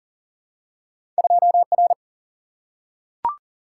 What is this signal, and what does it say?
Morse code, heard as a keyed tone, about 40 wpm, 700 Hz. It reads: 2R
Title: Single-Letter Single-Number 40wpm